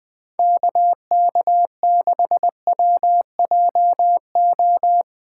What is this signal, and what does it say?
Morse code, heard as a keyed tone, about 21 wpm, 700 Hz. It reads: KK6WJO
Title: USA Callsign DE Callsign 20wpm